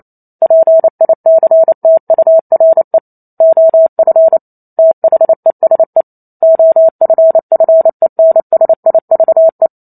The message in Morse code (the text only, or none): PICTURE OF THESE OFFENSIVE